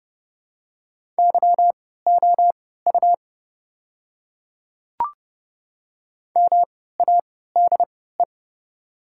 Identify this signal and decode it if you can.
Morse code, heard as a keyed tone, about 30 wpm, spaced as for 17 wpm, 700 Hz. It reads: YOU MADE
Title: 3x Character Spacing - Top 200 Words - 30wpm